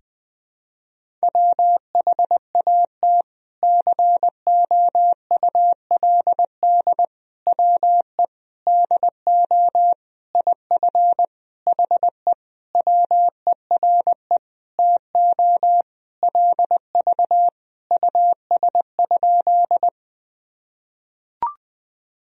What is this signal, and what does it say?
Morse code, heard as a keyed tone, about 20 wpm, 700 Hz. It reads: WHAT COULD WE DO IF HE WERE TO LV US?
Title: Sentences from Top 400 Words and Abbreviations 20wpm